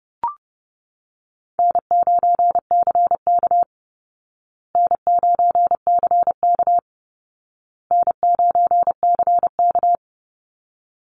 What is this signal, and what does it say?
Morse code, heard as a keyed tone, about 30 wpm, 700 Hz. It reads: N9CK N9CK N9CK
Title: Easy Call Signs Repeated 3x - 30wpm